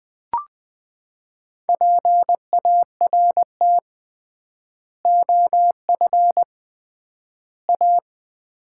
Morse code, heard as a keyed tone, about 20 wpm, 700 Hz. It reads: PART OF A
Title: 3x Word-Spacing - Sets of 3 Words - 20wpm